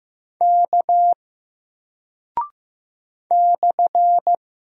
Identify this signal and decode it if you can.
Morse code, heard as a keyed tone, about 15 wpm, 700 Hz. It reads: K /
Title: Single Character (Letter:Number:Punctuation:Prosign) Alt 15wpm